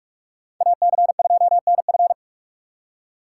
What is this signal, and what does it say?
Morse code, heard as a keyed tone, about 44 wpm, 700 Hz. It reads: AC2NF